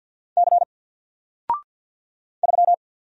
Morse code, heard as a keyed tone, about 50 wpm, 700 Hz. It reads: C 3
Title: Single Letter-Number 50wpm